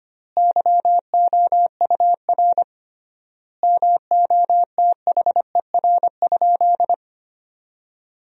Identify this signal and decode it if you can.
Morse code, heard as a keyed tone, about 25 wpm, 700 Hz. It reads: YOUR MOTHER?
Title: 3x Word-Spacing - Sentences from Top 200 Words - 25wpm